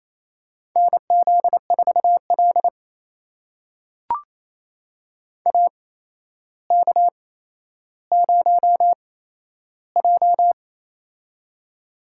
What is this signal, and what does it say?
Morse code, heard as a keyed tone, about 28 wpm, spaced as for 12 wpm, 700 Hz. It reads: NZ4L AK0J
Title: USA Call Signs - 4 Characters - 8x Character Spacing - 28wpm